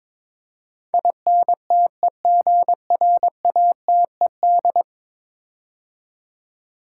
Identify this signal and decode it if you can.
Morse code, heard as a keyed tone, about 22 wpm, 700 Hz. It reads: INTEGRATED
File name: practice / 22wpm / 10 Letter Words - First 3 Letters Given - 22wpm